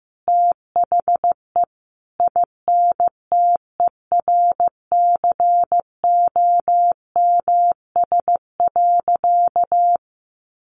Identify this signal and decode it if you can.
Morse code, heard as a keyed tone, about 15 wpm, 700 Hz. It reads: THE INTERCOMS.
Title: A Sound Of Thunder 15wpm